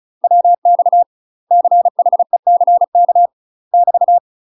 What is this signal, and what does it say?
Morse code, heard as a keyed tone, about 36 wpm, 700 Hz. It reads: WX CHECK <BT>
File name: UK MASTER QSOs TRAINING / 35WPM / UK MASTER QSOs 0009